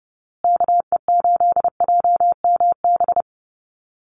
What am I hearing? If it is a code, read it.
Morse code, heard as a keyed tone, about 30 wpm, 700 Hz. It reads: KE8JMB
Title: USA Call Signs Repeated 3x 30wpm